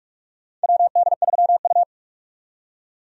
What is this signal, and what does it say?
Morse code, heard as a keyed tone, about 46 wpm, 700 Hz. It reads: WD3U
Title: USA Call Signs - 4 Characters - 45wpm